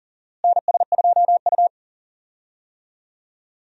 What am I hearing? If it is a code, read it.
Morse code, heard as a keyed tone, about 40 wpm, 700 Hz. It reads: NS2U
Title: USA Call Signs - 4 Characters - Repeated 3x - 40wpm